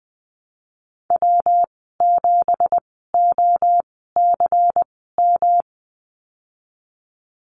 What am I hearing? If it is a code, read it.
Morse code, heard as a keyed tone, about 20 wpm, spaced as for 15 wpm, 700 Hz. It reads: W7OCM
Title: 179 Lesson 29g - Slash - Call Signs 20wpm